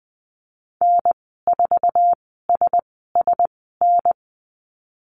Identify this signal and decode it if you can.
Morse code, heard as a keyed tone, about 20 wpm, spaced as for 15 wpm, 700 Hz. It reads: N4SSN
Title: 055 Lesson 10g - Letter r - Call Signs 20wpm